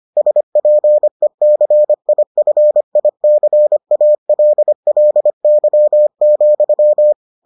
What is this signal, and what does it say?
Morse code, heard as a keyed tone, about 25 wpm, 600 Hz. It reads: SPECIFICALLY,